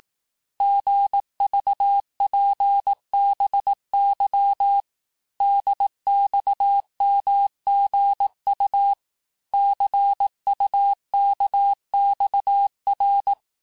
Morse code, heard as a keyed tone, about 18 wpm, 800 Hz. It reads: GVPBY DXMGU CUKXR
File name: cours-du-lundi files / cours-1852-texte-chiffre-vitesse-17MPM